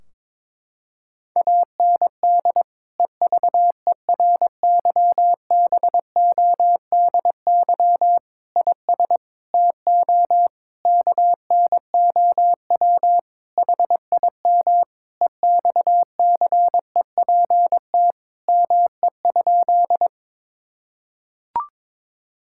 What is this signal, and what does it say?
Morse code, heard as a keyed tone, about 22 wpm, 700 Hz. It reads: AND EVERYBODY IS TO KNOW HIM EXCEPT ME?
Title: Sentences from Top 1600 Words 22wpm